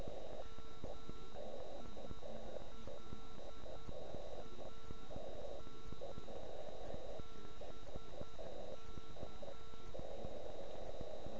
Morse code, heard as a keyed tone, about 9 wpm, 1340 Hz. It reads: MILAN BK